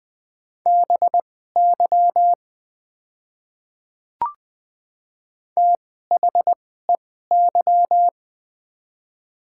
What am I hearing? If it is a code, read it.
Morse code, heard as a keyed tone, about 20 wpm, spaced as for 14 wpm, 700 Hz. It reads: BY THEY
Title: 2x Character Spacing - Top 100 Words - 20wpm